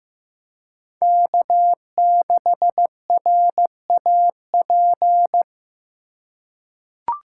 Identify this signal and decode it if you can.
Morse code, heard as a keyed tone, about 15 wpm, 700 Hz. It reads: K6RAP
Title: USA Call Signs - 5 Characters - 2x Character Spacing - 15wpm